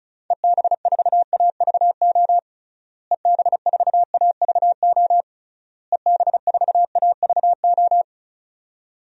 Morse code, heard as a keyed tone, about 36 wpm, 700 Hz. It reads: EB4AVO EB4AVO EB4AVO
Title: DXCC-100 Call Signs - 6 Characters - Repeated 3x - 35wpm